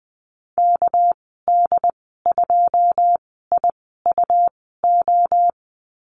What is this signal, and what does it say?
Morse code, heard as a keyed tone, about 20 wpm, spaced as for 14 wpm, 700 Hz. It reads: KD2IUO
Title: 213 Lesson 34g - Number 8 - Call Signs 20wpm